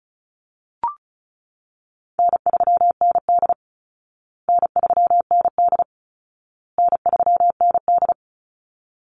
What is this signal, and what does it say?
Morse code, heard as a keyed tone, about 34 wpm, 700 Hz. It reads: N3ND N3ND N3ND
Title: Easy Call Signs Repeated 3x - 35wpm